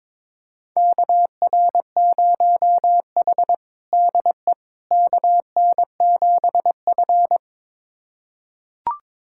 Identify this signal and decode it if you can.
Morse code, heard as a keyed tone, about 22 wpm, 700 Hz. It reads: KR0H DE KN7F
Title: USA Callsign DE Callsign - 4 Characters - 22wpm